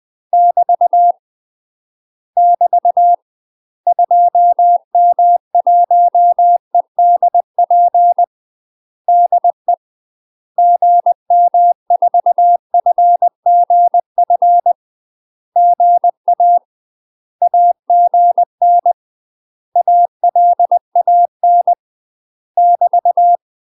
Morse code, heard as a keyed tone, about 20 wpm, 700 Hz. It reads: <BT> <BT> 2M1EDP DE GM4FGF GA AGN ALAN <BT>
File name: UK MASTER QSOs TRAINING / 20WPM / UK MASTER QSOs 0010